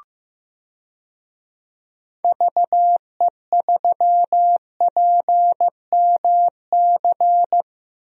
Morse code, heard as a keyed tone, about 15 wpm, 700 Hz. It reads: VE3PMC
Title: DXCC-100 Call Signs - 6 Characters - Repeated 3x - 15wpm